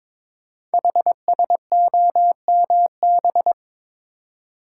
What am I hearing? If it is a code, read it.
Morse code, heard as a keyed tone, about 22 wpm, 700 Hz. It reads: HSOMB
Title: Code Groups - 5 Letters - 22wpm